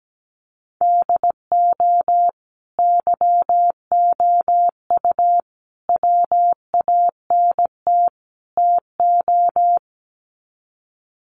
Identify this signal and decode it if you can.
Morse code, heard as a keyed tone, about 17 wpm, 700 Hz. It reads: DO YOU WANT TO
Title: Sets of 4 Words 17wpm